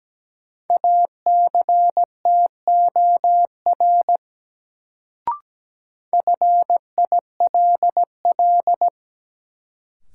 Morse code, heard as a keyed tone, about 17 wpm, 700 Hz. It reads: ACTOR FILL
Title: Sentences from Top 1500 Words - New Word Drill Alt - 17wpm